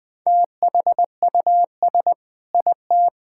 Morse code, heard as a keyed tone, about 21 wpm, 700 Hz. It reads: THUS IT
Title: Sentences from Top 1000 Words - Review Part 5 - 20wpm